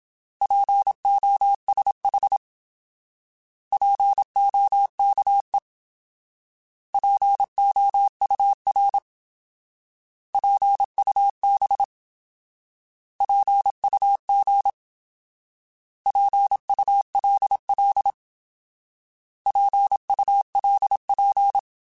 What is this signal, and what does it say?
Morse code, heard as a keyed tone, about 27 wpm, 800 Hz. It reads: POSH POKE POUR PUB PUG PULL PULP
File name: Alphabetical / Alphabetical mp3s / Ps @25